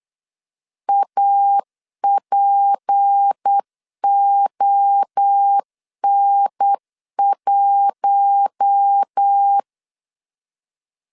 Morse code, heard as a keyed tone, about 8 wpm, 800 Hz. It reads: APON1
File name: LEZIONI / 10 SIGNS / 10C DOT 8wpm 3649